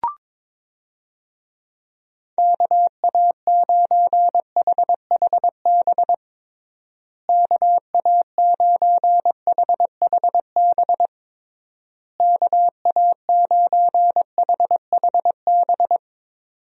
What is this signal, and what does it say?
Morse code, heard as a keyed tone, about 22 wpm, 700 Hz. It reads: KA9HHB KA9HHB KA9HHB
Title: USA Call Signs - 6 Characters - Repeated 3x - 22wpm